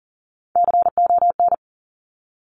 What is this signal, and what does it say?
Morse code, heard as a keyed tone, about 42 wpm, 690 Hz. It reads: CON
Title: Top Three Letter Combinations in Words 40wpm